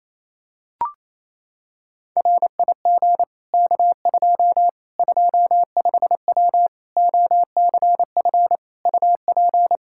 Morse code, heard as a keyed tone, about 28 wpm, 700 Hz. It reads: RIG K2 25W OCF UP
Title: Practice QSO - Exchange 2 - Info - 28wpm